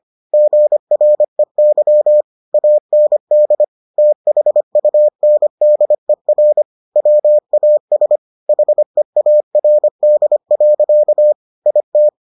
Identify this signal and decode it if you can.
Morse code, heard as a keyed tone, about 25 wpm, 600 Hz. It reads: GREY AND THUNDER WAS HEARD. IT